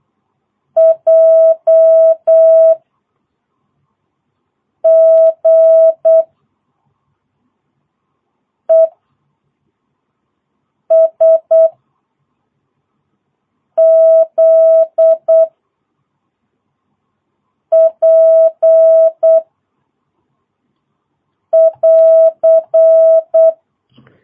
Morse code, heard as a keyed tone, about 8 wpm, 660 Hz. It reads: J G E S Z P <AR>
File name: jh8cbh / 2022cw